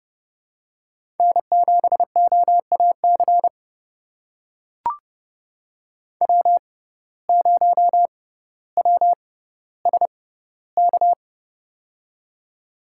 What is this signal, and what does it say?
Morse code, heard as a keyed tone, about 32 wpm, spaced as for 16 wpm, 700 Hz. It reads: N7OAC W0WSK
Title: USA Call Signs - 6x Character Spacing - 30wpm